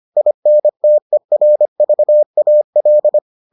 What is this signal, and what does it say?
Morse code, heard as a keyed tone, about 25 wpm, 600 Hz. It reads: INTERVAL